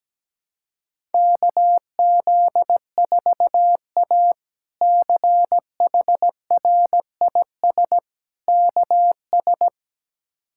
Morse code, heard as a keyed tone, about 17 wpm, 700 Hz. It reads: KZ4A CHRIS KS
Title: K1USN SST QSO Practice - 17wpm